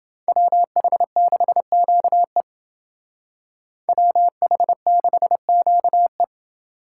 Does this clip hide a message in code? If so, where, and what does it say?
Yes: Morse code from 0.3 to 6.2 s, heard as a keyed tone, about 29 wpm, 700 Hz. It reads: WH6QE WH6QE